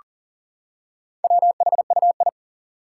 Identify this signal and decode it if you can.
Morse code, heard as a keyed tone, about 40 wpm, 700 Hz. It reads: WHUI
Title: Four Challenging Letter-Number Combinations 40wpm